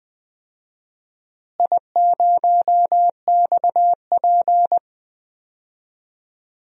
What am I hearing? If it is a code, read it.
Morse code, heard as a keyed tone, about 20 wpm, 700 Hz. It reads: I0XP
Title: DXCC-100 Call Signs - 4 Characters - Repeated 3x - 20wpm